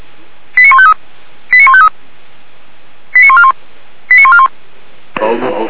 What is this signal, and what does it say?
Five-tone selective call (ZVEI), heard as quick runs of five tones, tones of about 70 ms, with speech. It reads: 89153 89153 89141 89141